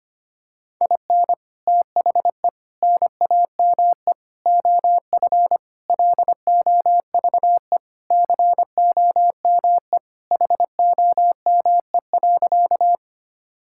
Morse code, heard as a keyed tone, about 25 wpm, 700 Hz. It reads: IN THE NAME OF LOVE COME HOME.
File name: practice / 25wpm / Sentences with 3-Word Phrases - 25wpm